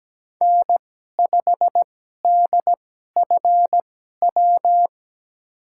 Morse code, heard as a keyed tone, about 17 wpm, spaced as for 12 wpm, 700 Hz. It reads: N5DFW
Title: USA Call Signs - 5 Characters - 2x Character Spacing - 17wpm